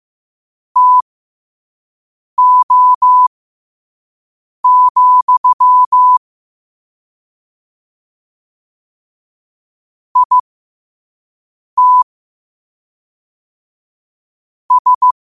Morse code, heard as a keyed tone, about 15 wpm, spaced as for 5 wpm, 1000 Hz. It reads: TO, IT S
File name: qotdfiles / 5wpm